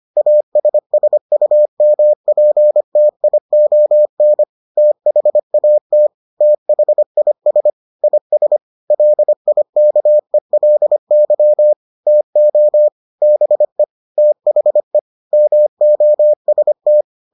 Morse code, heard as a keyed tone, about 25 wpm, 600 Hz. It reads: ASSUMPTION THAT THIS IS LIKELY TO BE THE MOST